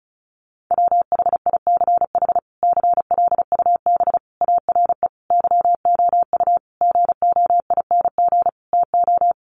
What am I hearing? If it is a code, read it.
Morse code, heard as a keyed tone, about 34 wpm, 700 Hz. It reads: WHICH CLUB ARE YOU GOING TO
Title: Sentences from Top 1200 Words 35wpm